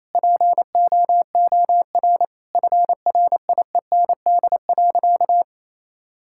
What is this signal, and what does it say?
Morse code, heard as a keyed tone, about 28 wpm, 700 Hz. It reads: POOR FRIEND.
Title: Sentences from Top 1000 Words - Review Part 5 - 28wpm